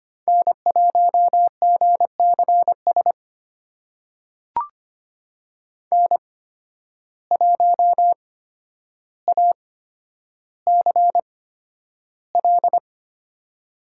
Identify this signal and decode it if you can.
Morse code, heard as a keyed tone, about 25 wpm, spaced as for 11 wpm, 700 Hz. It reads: N1GCS N1ACL
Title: USA Call Signs - 5 Characters - 8x Character Spacing - 25wpm